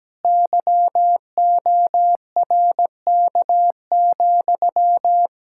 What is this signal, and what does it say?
Morse code, heard as a keyed tone, about 17 wpm, 700 Hz. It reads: YORK,
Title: Practice QSO - Exchange 1 - Beginning - 17wpm